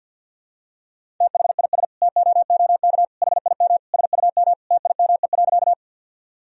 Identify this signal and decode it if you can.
Morse code, heard as a keyed tone, about 47 wpm, 700 Hz. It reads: THIS TOOK HIM SUM TIME.
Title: Sentences from Top 300 Words and Abbreviations 50wpm